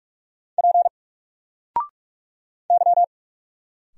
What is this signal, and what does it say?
Morse code, heard as a keyed tone, about 45 wpm, 700 Hz. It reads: P Y
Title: Single Letter-Number 45wpm